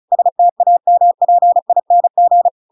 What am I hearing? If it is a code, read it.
Morse code, heard as a keyed tone, about 34 wpm, 700 Hz. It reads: STAMPING